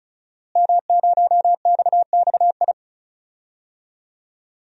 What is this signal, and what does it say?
Morse code, heard as a keyed tone, about 36 wpm, 700 Hz. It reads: M0XXI
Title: DXCC-100 Call Signs - 5 Characters - Repeated 3x - 35wpm